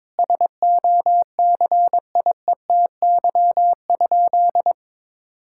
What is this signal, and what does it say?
Morse code, heard as a keyed tone, about 22 wpm, 700 Hz. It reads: SOCIETY?
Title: Sentences from Top 1000 Words - Review Part 6 - 22wpm